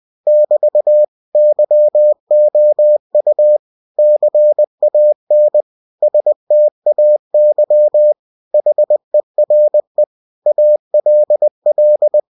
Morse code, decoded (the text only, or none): <BT> YOU CAN STAY HERE ALL